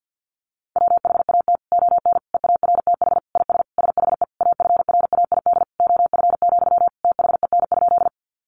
Morse code, heard as a keyed tone, about 48 wpm, 700 Hz. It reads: WHAT ON EARTH IS SHE AFRAID OF, THEN?
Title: Sentences from Top 1500 Words 50wpm